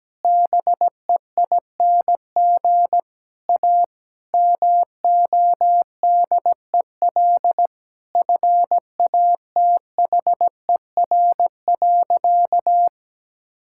Morse code, heard as a keyed tone, about 17 wpm, 700 Hz. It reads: BEING A MODEL FATHER.
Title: Sentences from Top 1000 Words - Review Part 5 - 17wpm